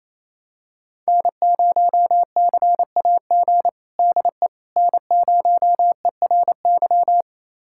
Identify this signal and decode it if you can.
Morse code, heard as a keyed tone, about 28 wpm, 700 Hz. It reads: N0CAG DE N0ERY
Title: USA Callsign DE Callsign - 5 Characters - 28wpm